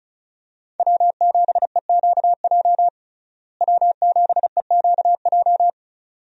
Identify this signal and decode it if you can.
Morse code, heard as a keyed tone, about 34 wpm, 700 Hz. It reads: W7EQJ W7EQJ
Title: USA Call Signs - 5 Characters - Repeated 2x - 35wpm